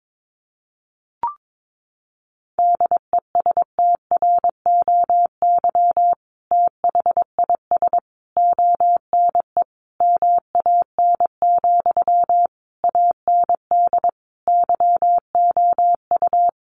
Morse code, heard as a keyed tone, about 22 wpm, 700 Hz. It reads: DESTROY THIS ONE MAN, AND YOU
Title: A Sound Of Thunder 22wpm